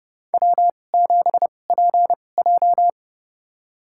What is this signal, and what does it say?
Morse code, heard as a keyed tone, about 30 wpm, spaced as for 22 wpm, 700 Hz. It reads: W7PJ
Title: USA Call Signs - 4 Characters - 2x Character Spacing - 30wpm